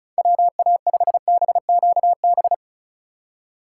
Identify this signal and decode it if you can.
Morse code, heard as a keyed tone, about 36 wpm, 700 Hz. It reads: WA5BQB
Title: USA Call Signs - 6 Characters - 4x Character Spacing - 35wpm